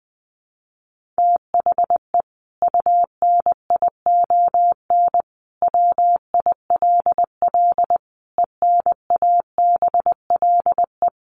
Morse code, heard as a keyed tone, about 19 wpm, 700 Hz. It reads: THE UNION WILL ENABLE
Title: Sentences from Top 1500 Words 20wpm